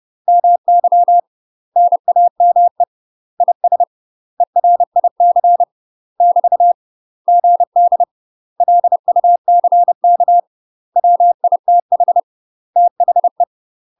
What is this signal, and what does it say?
Morse code, heard as a keyed tone, about 30 wpm, 700 Hz. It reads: MY NAME IS ERIC <BT> GD LUCK WITH THE
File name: UK MASTER QSOs TRAINING / 30WPM / UK MASTER QSOs 0033